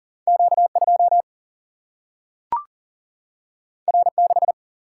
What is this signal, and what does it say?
Morse code, heard as a keyed tone, about 39 wpm, 700 Hz. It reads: Q2 R6
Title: Single-Letter Single-Number 40wpm